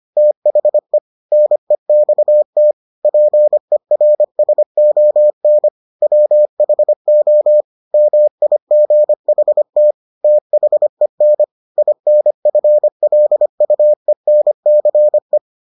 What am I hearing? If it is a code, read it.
Morse code, heard as a keyed tone, about 25 wpm, 600 Hz. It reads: THE NEXT PERSON WHO MIGHT THEN INFLUENCE